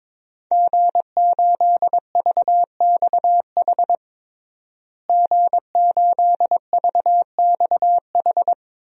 Morse code, heard as a keyed tone, about 22 wpm, 700 Hz. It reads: G8VXH G8VXH
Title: DXCC-100 Call Signs - 5 Characters - Repeated 2x - 22wpm